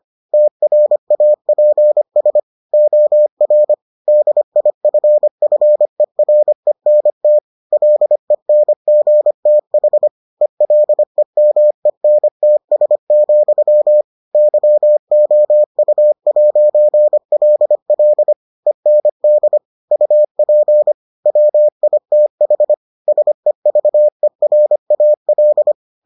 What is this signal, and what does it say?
Morse code, heard as a keyed tone, about 25 wpm, 600 Hz. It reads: TRAPS OR DIFFERENT LENGTH ELEMENTS, YOU'LL END UP WITH SEVERAL